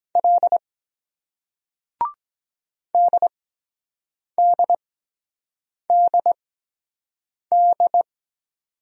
Morse code, heard as a keyed tone, about 23 wpm, spaced as for 7 wpm, 700 Hz. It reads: L DDDD